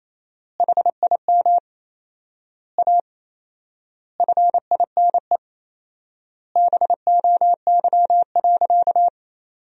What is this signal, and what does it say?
Morse code, heard as a keyed tone, about 28 wpm, 700 Hz. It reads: HIM A FINE BOY.
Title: 4x Word-Spacing - Sentences from Top 500 Words - 28wpm